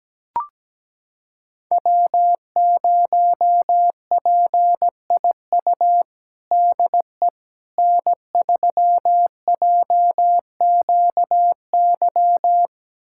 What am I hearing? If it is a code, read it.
Morse code, heard as a keyed tone, about 17 wpm, 700 Hz. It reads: W0PIU DE N3JQY
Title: USA Callsign DE Callsign - 5 Characters - 17wpm